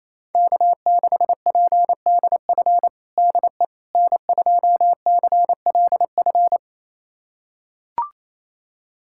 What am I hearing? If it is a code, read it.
Morse code, heard as a keyed tone, about 28 wpm, 700 Hz. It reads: K6PDF DE N2CLF
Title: USA Callsign DE Callsign - 5 Characters - 28wpm